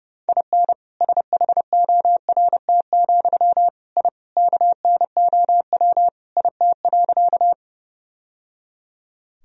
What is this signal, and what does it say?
Morse code, heard as a keyed tone, about 32 wpm, 700 Hz. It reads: IN SHORT, I KNOW IT.
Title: Sentences from Top 1000 Words - Review Part 2 - 30wpm